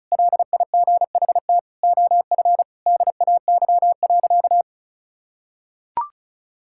Morse code, heard as a keyed tone, about 36 wpm, 700 Hz. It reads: LIGHT OF DAY.
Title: Sentences from Top 1000 Words - Review Part 5 - 35wpm